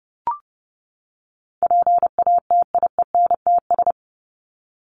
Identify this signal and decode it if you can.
Morse code, heard as a keyed tone, about 30 wpm, 700 Hz. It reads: PATIENTS
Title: ES:S Suffix - Top 100 Words with Base Words 30wpm